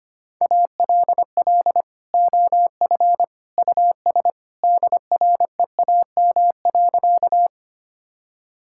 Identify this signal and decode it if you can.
Morse code, heard as a keyed tone, about 25 wpm, 700 Hz. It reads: ALL OF US DREAM.
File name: practice / 25wpm / Sentences with 2-Word Phrases - 25wpm